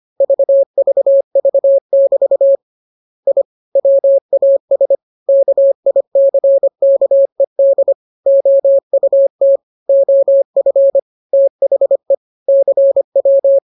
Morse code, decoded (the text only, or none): VVV<BT> I WAS KICKED OUT OF THE CW